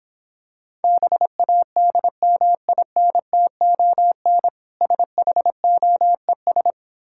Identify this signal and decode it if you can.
Morse code, heard as a keyed tone, about 26 wpm, 700 Hz. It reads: BADMINTON SHOES